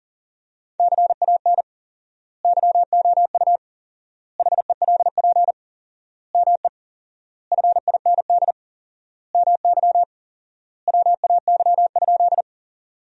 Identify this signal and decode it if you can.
Morse code, heard as a keyed tone, about 40 wpm, 700 Hz. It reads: CAN YOU HELP ME FIND MY WAY?
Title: 4x Word-Spacing - Sentences from Top 200 Words - 40wpm